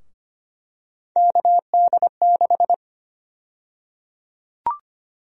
Morse code, heard as a keyed tone, about 25 wpm, 700 Hz. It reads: KD6
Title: USA Call Sign Prefixes - Repeated 2x - 25wpm